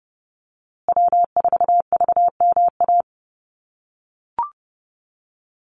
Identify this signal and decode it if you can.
Morse code, heard as a keyed tone, about 30 wpm, 700 Hz. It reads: W4VMA
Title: USA Call Signs Repeated 2x 30wpm